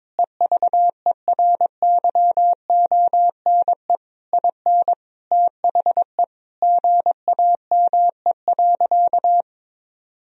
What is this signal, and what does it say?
Morse code, heard as a keyed tone, about 21 wpm, 700 Hz. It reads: EVERYONE IN THE GAME.
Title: Sentences from Top 1000 Words - Review Part 4 - 22wpm